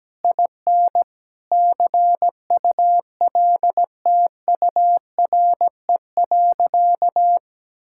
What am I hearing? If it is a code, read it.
Morse code, heard as a keyed tone, about 17 wpm, 700 Hz. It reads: IN CULTURE.
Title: Sentences from Top 1000 Words - Review Part 6 - 17wpm